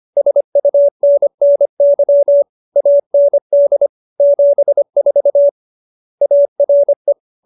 Morse code, heard as a keyed tone, about 25 wpm, 570 Hz. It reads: SUNNY AND 74 ARE
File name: QSO312 25